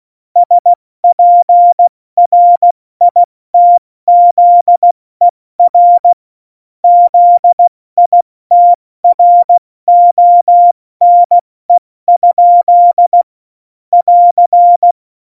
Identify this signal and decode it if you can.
Morse code, heard as a keyed tone, about 16 wpm, 700 Hz. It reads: SPRITZER ZITRONE? <AR>
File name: Morsekurs / Witze / Witz 13. Februar 2018, 70 F80